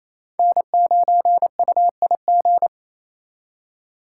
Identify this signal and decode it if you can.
Morse code, heard as a keyed tone, about 35 wpm, 700 Hz. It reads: N9UIG